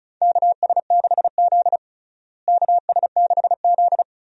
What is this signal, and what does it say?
Morse code, heard as a keyed tone, about 36 wpm, 700 Hz. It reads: KS6Z KS6Z
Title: USA Call Signs - 4 Characters - Repeated 2x - 35wpm